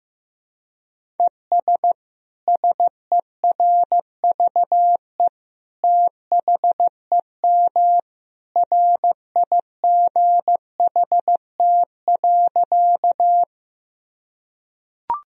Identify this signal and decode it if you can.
Morse code, heard as a keyed tone, about 16 wpm, 700 Hz. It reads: ES SERVE THEM RIGHT.
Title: Sentences from Top 400 Words and Abbreviations 15wpm